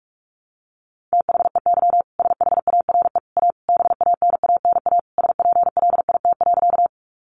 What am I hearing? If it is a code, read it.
Morse code, heard as a keyed tone, about 44 wpm, 700 Hz. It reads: THEY SHARE A BANANA SPLIT.
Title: Additional Easy Phrases 45wpm